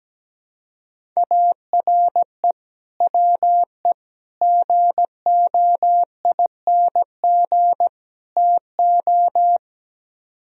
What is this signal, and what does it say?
Morse code, heard as a keyed tone, about 17 wpm, 700 Hz. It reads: ARE WE GOING TO
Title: Speed-Racing - Sets of 4 Words 17wpm